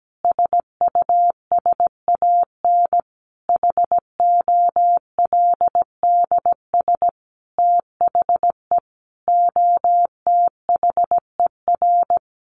Morse code, heard as a keyed tone, about 17 wpm, 700 Hz. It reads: SUSAN HOLDS THE OTHER
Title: Easy Phrases 17wpm